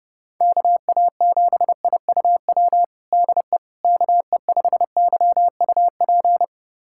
Morse code, heard as a keyed tone, about 28 wpm, 700 Hz. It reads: KA7IUW DE KE5YUP
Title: USA Callsign DE Callsign - 6 Characters - 30wpm